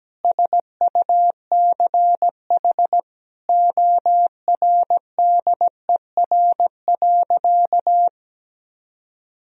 Morse code, heard as a keyed tone, about 17 wpm, 700 Hz. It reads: SUCH ORDER.
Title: Sentences from Top 1000 Words - Review Part 2 - 17wpm